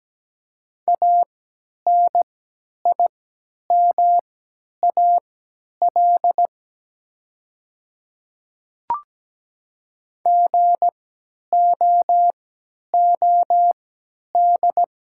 Morse code, heard as a keyed tone, about 17 wpm, spaced as for 10 wpm, 700 Hz. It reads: ANIMAL GOOD
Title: 3x Character Spacing - Top 300 Words - 17wpm